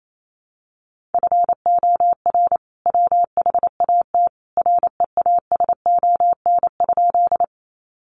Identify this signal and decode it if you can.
Morse code, heard as a keyed tone, about 28 wpm, 700 Hz. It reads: FOR WHAT REASON?
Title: Sentences from Top 700 Words 28wpm